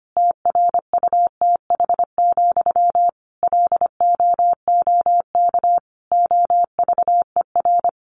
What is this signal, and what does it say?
Morse code, heard as a keyed tone, about 25 wpm, 700 Hz. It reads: TRUTH, LOOK OVER
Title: Sentences from Top 800 Words 25wpm